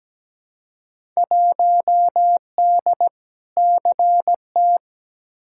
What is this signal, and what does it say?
Morse code, heard as a keyed tone, about 17 wpm, 700 Hz. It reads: 1D CT
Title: Field Day Exchanges - 17wpm